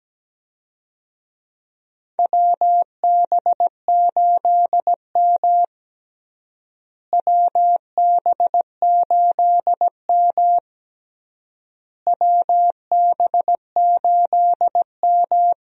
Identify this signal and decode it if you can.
Morse code, heard as a keyed tone, about 17 wpm, 700 Hz. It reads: WB8M WB8M WB8M
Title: USA Call Signs - 4 Characters - Repeated 3x - 17wpm